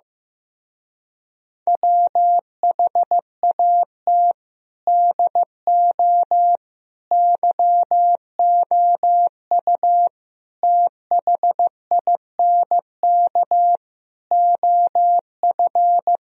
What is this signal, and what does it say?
Morse code, heard as a keyed tone, about 15 wpm, 700 Hz. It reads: WHAT DO YOU THINK OF
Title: Speed-Racing - Sets of 5 Words 15wpm